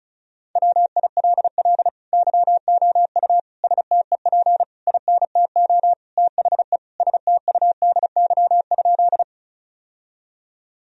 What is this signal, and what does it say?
Morse code, heard as a keyed tone, about 36 wpm, 700 Hz. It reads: WILL YOU STEP INTO THE STUDY?
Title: Sentences from Top 1000 Words - Review Part 2 - 35wpm